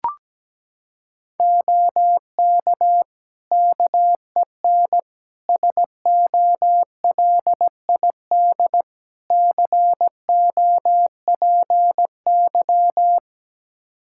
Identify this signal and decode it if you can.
Morse code, heard as a keyed tone, about 17 wpm, 700 Hz. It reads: OK KEN SOLID COPY
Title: Practice QSO - Exchange 3 - More Info - 17wpm